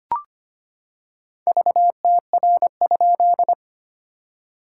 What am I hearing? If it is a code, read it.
Morse code, heard as a keyed tone, about 25 wpm, 700 Hz. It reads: VTR?
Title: SOTA QSO Practice 25wpm